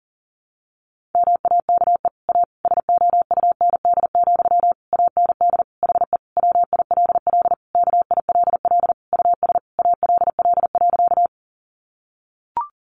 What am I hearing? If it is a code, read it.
Morse code, heard as a keyed tone, about 40 wpm, 700 Hz. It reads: MAKE A SOUND, AND HE WILL KILL US ALL.
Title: Sentences from Top 1400 Words 40wpm